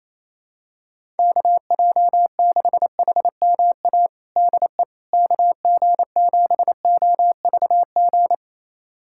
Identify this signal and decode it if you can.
Morse code, heard as a keyed tone, about 28 wpm, 700 Hz. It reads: KJ6HMA DE KG7OVG
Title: USA Callsign DE Callsign - 6 Characters - 28wpm